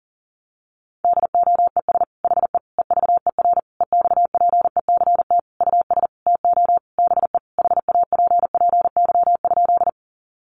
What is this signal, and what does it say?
Morse code, heard as a keyed tone, about 40 wpm, 700 Hz. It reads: DOES HE EVER EXPECT US TO BE HAPPY?
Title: Sentences from Top 1100 Words 40wpm